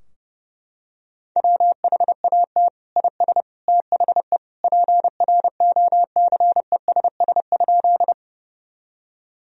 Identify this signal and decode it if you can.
Morse code, heard as a keyed tone, about 30 wpm, 700 Hz. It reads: WHAT IS THE PROCESS?
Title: Sentences from Top 1000 Words - Review Part 3 - 30wpm